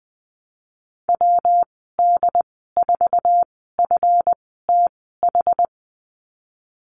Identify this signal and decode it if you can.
Morse code, heard as a keyed tone, about 20 wpm, spaced as for 15 wpm, 700 Hz. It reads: WD4FTH
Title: 154 Lesson 25g - Letter p - Call Signs 20wpm